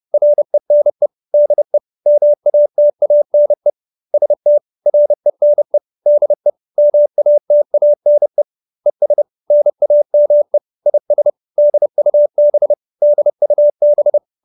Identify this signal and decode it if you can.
Morse code, heard as a keyed tone, about 30 wpm, 590 Hz. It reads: RENE DE MATANE ST RENE DE MATANE ES NAME IS DUB DUB
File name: QSO314 30